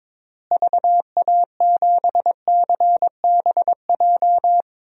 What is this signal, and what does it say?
Morse code, heard as a keyed tone, about 22 wpm, 700 Hz. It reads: VA7CBJ
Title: DXCC-100 Call Signs - 6 Characters - Repeated 3x - 22wpm